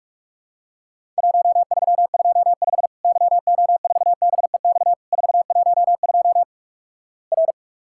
Morse code, heard as a keyed tone, about 44 wpm, 700 Hz. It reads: 1325 YO4BEX 412 R
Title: Worked All Europe DX Contest - QTC Exchange - 45wpm